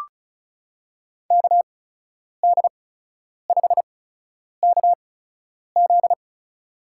Morse code, heard as a keyed tone, about 36 wpm, spaced as for 10 wpm, 700 Hz. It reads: KD5KZ
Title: USA Call Signs - 5 Characters - 8x Character Spacing - 35wpm